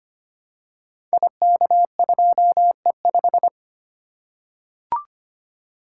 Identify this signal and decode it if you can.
Morse code, heard as a keyed tone, about 25 wpm, 700 Hz. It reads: IK2E5
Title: Code Groups - 5 Letter:Numbers - 25wpm